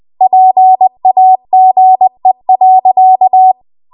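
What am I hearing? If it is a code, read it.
Morse code, heard as a keyed tone, about 20 wpm, 750 Hz. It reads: PAGE.
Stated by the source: Morse code practice at 20 WPM.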